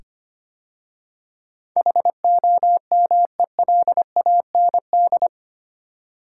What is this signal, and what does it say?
Morse code, heard as a keyed tone, about 25 wpm, 700 Hz. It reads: HOMELAND
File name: 8 Letter Words - First 3 Letters Given - 25wpm